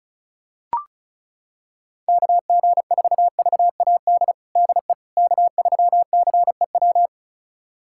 Morse code, heard as a keyed tone, about 34 wpm, 700 Hz. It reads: KG4VAD DE K3CEW
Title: USA Callsign DE Callsign 35wpm